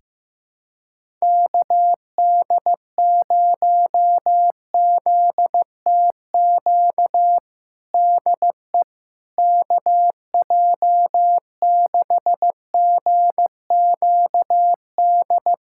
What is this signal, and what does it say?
Morse code, heard as a keyed tone, about 16 wpm, 700 Hz. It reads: KD0ZTQ DE KJ6GQD
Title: USA Callsign DE Callsign 15wpm